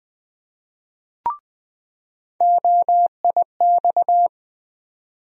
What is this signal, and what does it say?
Morse code, heard as a keyed tone, about 20 wpm, 700 Hz. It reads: OIX
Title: USA Call Sign Postfixes - 3 Letters 20wpm